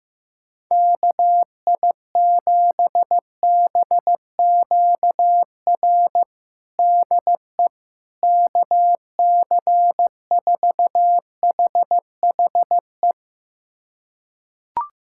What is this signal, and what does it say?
Morse code, heard as a keyed tone, about 15 wpm, 700 Hz. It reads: KI7BQR DE KC4HHE
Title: USA Callsign DE Callsign 15wpm